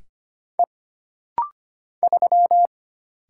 Morse code, heard as a keyed tone, about 25 wpm, 700 Hz. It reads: E 3
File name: Single Character (Letter:Number:Punctuation:Prosign) - Rapid-Fire - 25wpm